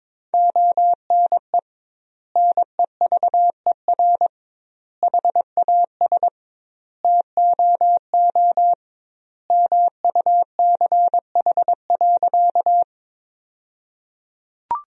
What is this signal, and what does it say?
Morse code, heard as a keyed tone, about 22 wpm, 700 Hz. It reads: ONE NEVER HAS TOO MUCH.
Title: 2x Word-Spacing - Sentences from Top 300 Words - 22wpm